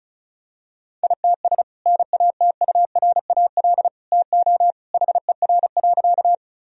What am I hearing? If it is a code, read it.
Morse code, heard as a keyed tone, about 34 wpm, 700 Hz. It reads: ITS NATURAL TO HER.
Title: Sentences from Top 1000 Words - Review Part 4 - 35wpm